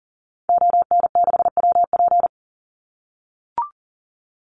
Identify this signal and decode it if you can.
Morse code, heard as a keyed tone, about 40 wpm, 700 Hz. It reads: ON6WP
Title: Call Signs - 40wpm